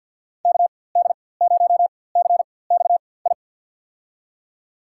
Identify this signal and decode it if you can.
Morse code, heard as a keyed tone, about 46 wpm, spaced as for 23 wpm, 700 Hz. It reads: KD0CXI
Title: USA Call Signs - 6 Characters - 4x Character Spacing - 50wpm